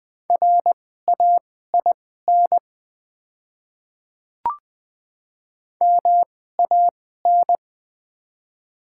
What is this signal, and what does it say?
Morse code, heard as a keyed tone, about 20 wpm, spaced as for 14 wpm, 700 Hz. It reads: RAIN MAN
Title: 2x Character Spacing - Top 300 Words - 20wpm